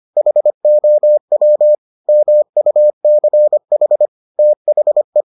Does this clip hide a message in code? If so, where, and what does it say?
Yes: Morse code from 0.2 to 5.2 s, heard as a keyed tone, about 25 wpm, 600 Hz. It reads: HOW MUCH THE